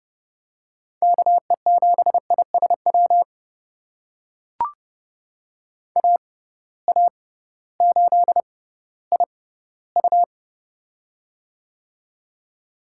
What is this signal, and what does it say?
Morse code, heard as a keyed tone, about 30 wpm, spaced as for 16 wpm, 700 Hz. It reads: KE7ISW AA8IU
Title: USA Call Signs - 6x Character Spacing - 30wpm